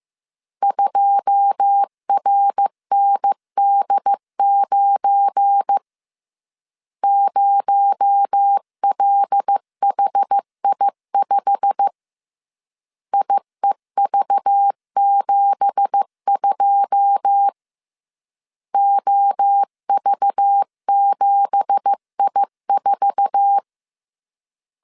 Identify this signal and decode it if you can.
Morse code, heard as a keyed tone, about 15 wpm, 800 Hz. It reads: 2RND9 0LHI5 IEV72 OV7I4
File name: CW / DET / 1 9 1500C S A 14wpm 3222